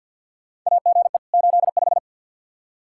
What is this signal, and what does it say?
Morse code, heard as a keyed tone, about 52 wpm, 700 Hz. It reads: AGE 85
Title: Practice QSO - Exchange 3 - More Info - 50wpm